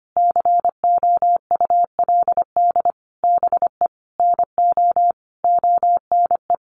Morse code, heard as a keyed tone, about 25 wpm, 700 Hz. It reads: COULD BE NO ONE
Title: Sentences from Top 800 Words 25wpm